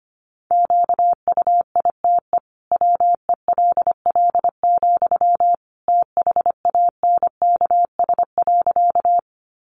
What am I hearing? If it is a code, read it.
Morse code, heard as a keyed tone, about 25 wpm, 700 Hz. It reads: QUITE WELL, THANKS.
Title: Sentences from Top 700 Words 25wpm